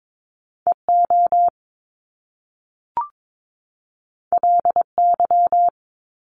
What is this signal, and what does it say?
Morse code, heard as a keyed tone, about 22 wpm, 700 Hz. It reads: EO LY
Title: Top Two Letter Combinations in Words 22wpm